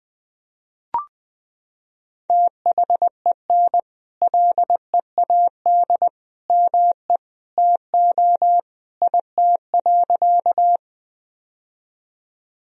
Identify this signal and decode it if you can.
Morse code, heard as a keyed tone, about 20 wpm, 700 Hz. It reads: THEN LEAD ME TO IT.
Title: Sentences from Top 1000 Words - Review Part 2 - 20wpm